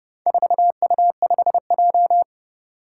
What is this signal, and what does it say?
Morse code, heard as a keyed tone, about 32 wpm, 700 Hz. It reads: 4U5J
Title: Four Challenging Letter-Number Combinations 30wpm